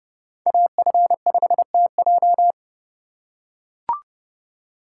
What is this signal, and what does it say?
Morse code, heard as a keyed tone, about 30 wpm, 700 Hz. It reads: AF5TJ